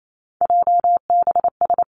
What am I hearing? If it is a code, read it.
Morse code, heard as a keyed tone, about 28 wpm, 700 Hz. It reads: JBS
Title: Three Challenging Letter-Number Combinations 28wpm